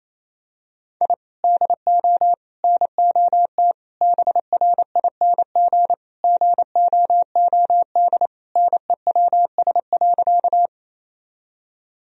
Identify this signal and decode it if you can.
Morse code, heard as a keyed tone, about 28 wpm, 700 Hz. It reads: I DO NOT BRING GOOD NEWS.
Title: Sentences from Top 1000 Words - Review Part 3 - 28wpm